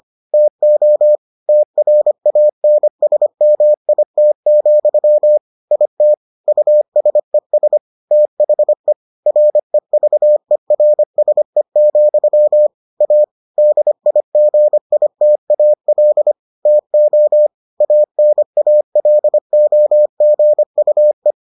Morse code, heard as a keyed tone, about 25 wpm, 600 Hz. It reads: TO TRANSMIT, IT USES THE REVERSE, A DIGITAL TO ANALOGUE